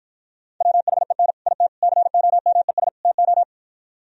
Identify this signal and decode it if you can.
Morse code, heard as a keyed tone, about 52 wpm, 700 Hz. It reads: WHEN IT COMES TO